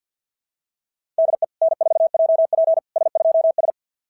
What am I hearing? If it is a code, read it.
Morse code, heard as a keyed tone, about 46 wpm, 660 Hz. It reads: DE N4JP S2S
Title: SOTA QSO Practice 50wpm